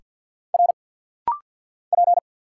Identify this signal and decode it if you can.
Morse code, heard as a keyed tone, about 49 wpm, 700 Hz. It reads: R P
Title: Single Letter-Number - Rapid-Fire 50wpm